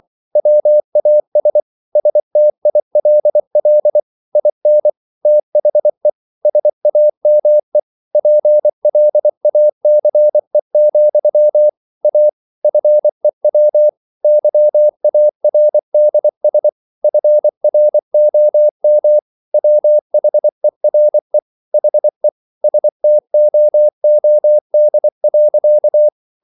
Morse code, heard as a keyed tone, about 24 wpm, 600 Hz. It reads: WAS STILL IN THE SAME PLACE, A FEW YARDS FROM WHERE HE STOOD.